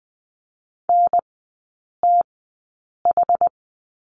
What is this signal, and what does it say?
Morse code, heard as a keyed tone, about 20 wpm, spaced as for 8 wpm, 700 Hz. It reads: NTH
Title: 060 Lesson 11e - Letter h - Three Characters 20wpm